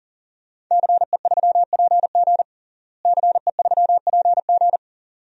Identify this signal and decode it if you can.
Morse code, heard as a keyed tone, about 40 wpm, 700 Hz. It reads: CE3PG CE3PG
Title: DXCC-100 Call Signs - 5 Characters - Repeated 2x - 40wpm